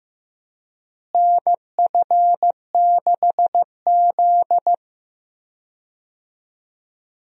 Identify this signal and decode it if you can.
Morse code, heard as a keyed tone, about 15 wpm, 700 Hz. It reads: NF6Z
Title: USA Call Signs - 4 Characters - Repeated 3x - 15wpm